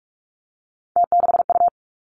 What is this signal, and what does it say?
Morse code, heard as a keyed tone, about 46 wpm, 700 Hz. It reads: T6U
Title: Three Challenging Letter-Number Combinations 45wpm